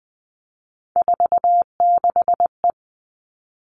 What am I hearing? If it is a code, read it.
Morse code, heard as a keyed tone, about 20 wpm, 700 Hz. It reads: 46E
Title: All Cut Numbers - 3-Digit 20wpm